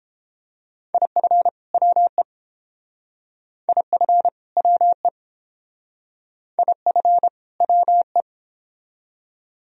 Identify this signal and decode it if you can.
Morse code, heard as a keyed tone, about 29 wpm, 700 Hz. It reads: IF WE IF WE IF WE